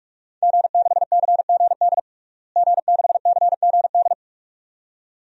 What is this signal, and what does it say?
Morse code, heard as a keyed tone, about 46 wpm, 700 Hz. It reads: G6CGD G6CGD
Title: DXCC-100 Call Signs - 5 Characters - Repeated 2x - 45wpm